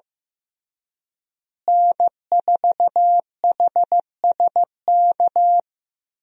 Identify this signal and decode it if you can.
Morse code, heard as a keyed tone, about 15 wpm, 700 Hz. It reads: N4HSK
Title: USA Call Signs - 5 Characters - Repeated 3x - 15wpm